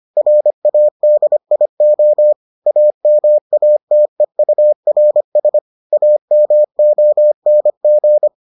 Morse code, heard as a keyed tone, about 25 wpm, 600 Hz. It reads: RADIO AMATEURS AMONG